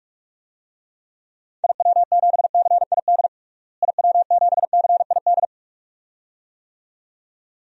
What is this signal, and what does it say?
Morse code, heard as a keyed tone, about 46 wpm, 700 Hz. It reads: IW7CID IW7CID
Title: DXCC-100 Call Signs - 6 Characters - Repeated 2x - 45wpm